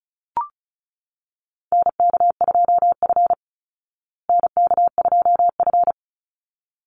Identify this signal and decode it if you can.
Morse code, heard as a keyed tone, about 34 wpm, 700 Hz. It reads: NK2F NK2F
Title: Call Signs Repeated 3x - 35wpm